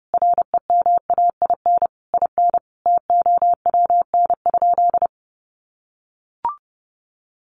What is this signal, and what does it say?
Morse code, heard as a keyed tone, about 30 wpm, 700 Hz. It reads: REMAIN IN TOWN?
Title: Sentences from Top 1200 Words 30wpm